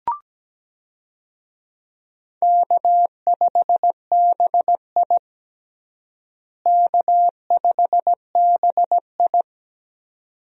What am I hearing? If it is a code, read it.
Morse code, heard as a keyed tone, about 17 wpm, 700 Hz. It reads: K5BI K5BI
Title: USA Call Signs - 4 Characters - Repeated 2x - 17wpm